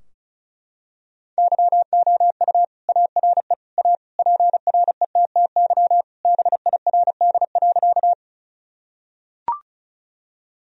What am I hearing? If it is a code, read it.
Morse code, heard as a keyed tone, about 34 wpm, 700 Hz. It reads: YOU ARE A PRETTY BIRD.
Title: Sentences from Top 1000 Words - Review Part 2 - 35wpm